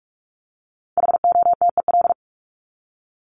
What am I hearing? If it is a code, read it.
Morse code, heard as a keyed tone, about 44 wpm, 700 Hz. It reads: HOTEL
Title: Top 1501-1750 Words 45wpm